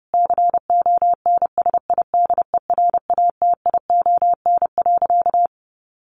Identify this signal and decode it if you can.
Morse code, heard as a keyed tone, about 28 wpm, 700 Hz. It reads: CONSIDERATION.
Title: Sentences from Top 1500 Words 30wpm